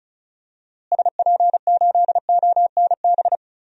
Morse code, heard as a keyed tone, about 36 wpm, 700 Hz. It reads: SP8ONB
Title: DXCC-100 Call Signs - 6 Characters - Repeated 3x - 35wpm